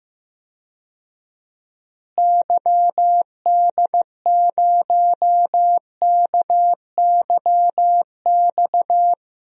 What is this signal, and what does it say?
Morse code, heard as a keyed tone, about 15 wpm, 700 Hz. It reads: YD0KYX
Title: DXCC-100 Call Signs - 6 Characters - Repeated 2x - 15wpm